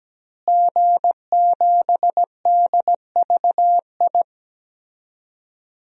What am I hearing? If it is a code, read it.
Morse code, heard as a keyed tone, about 17 wpm, 700 Hz. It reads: G7DVI
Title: DXCC-100 Call Signs - 5 Characters - Repeated 3x - 17wpm